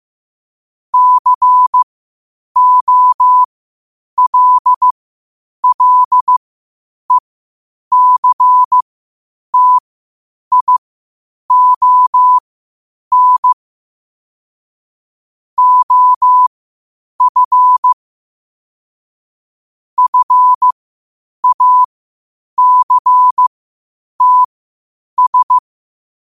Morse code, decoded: COLLECTION OF FACTS

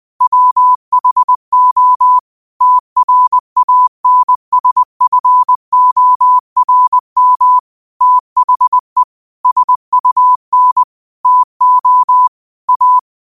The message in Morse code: WHO TRANSFORM THE SUN TO A

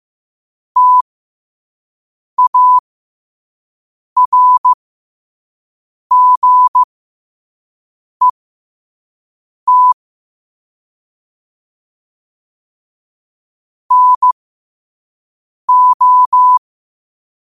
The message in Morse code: TARGET NO

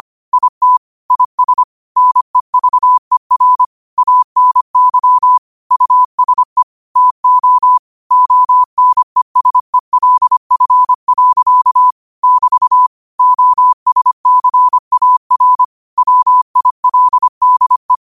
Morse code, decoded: IT IS NEVER ANY USE TO ONESELF. <BT> OSCAR WILDE

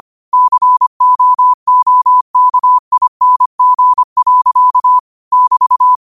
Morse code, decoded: COOKING. <BT>

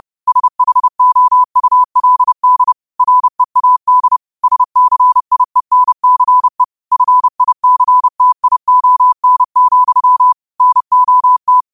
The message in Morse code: SHOULD READ SCIENCE FICTION, NOT

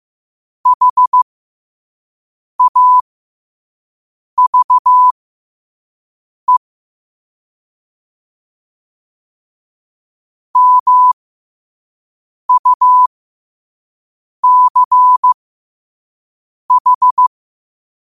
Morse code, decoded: HAVE MUCH

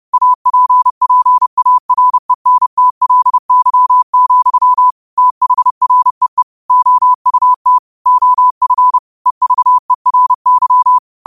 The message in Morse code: APPARENTLY, THREE OUT OF EVERY

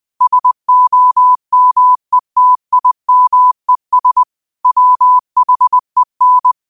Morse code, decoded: SOMETIMES WHEN

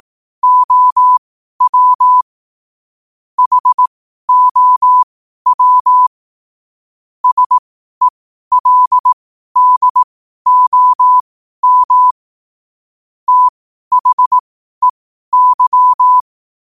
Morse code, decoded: OW HOW SELDOM THEY

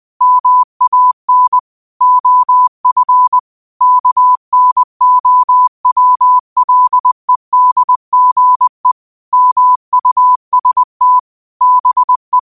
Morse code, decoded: MAN OF KNOWLEDGE MUST BE

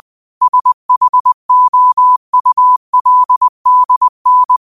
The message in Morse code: SHOULDN

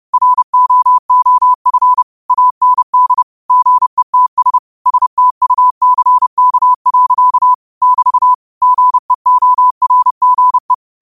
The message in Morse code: ROOF AND GETS STUCK. <BT> GEORGE

